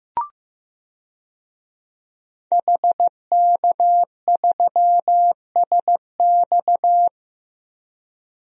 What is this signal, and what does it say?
Morse code, heard as a keyed tone, about 15 wpm, 700 Hz. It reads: HK3SX
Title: DXCC-100 Call Signs - 5 Characters - 15wpm